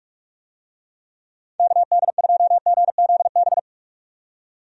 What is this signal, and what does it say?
Morse code, heard as a keyed tone, about 47 wpm, 700 Hz. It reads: KD2GZB
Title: USA Call Signs - 6 Characters - 45wpm